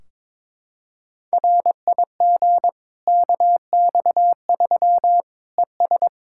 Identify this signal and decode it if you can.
Morse code, heard as a keyed tone, about 22 wpm, 700 Hz. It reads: RIG KX3 ES
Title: K1USN SST QSO Practice 22wpm